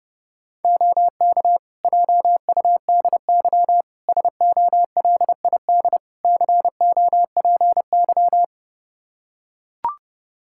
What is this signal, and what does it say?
Morse code, heard as a keyed tone, about 31 wpm, 700 Hz. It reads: OK JUDY SOLID COPY
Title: Practice QSO - Exchange 3 - More Info - 30wpm